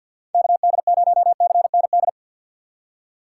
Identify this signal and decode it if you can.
Morse code, heard as a keyed tone, about 49 wpm, 700 Hz. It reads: KD0CND
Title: USA Call Signs - 6 Characters - 4x Character Spacing - 50wpm